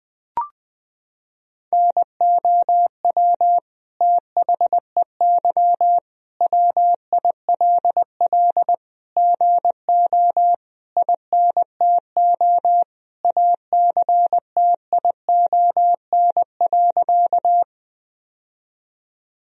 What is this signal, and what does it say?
Morse code, heard as a keyed tone, about 21 wpm, 700 Hz. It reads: NOW THEY WILL GO INTO ACTION.
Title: Sentences from Top 1000 Words - Review Part 4 - 20wpm